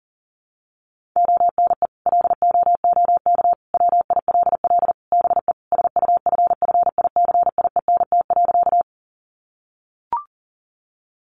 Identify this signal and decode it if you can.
Morse code, heard as a keyed tone, about 40 wpm, 700 Hz. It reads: ONE LOOK WILL BE SUFFICIENT.
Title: Sentences from Top 1400 Words 40wpm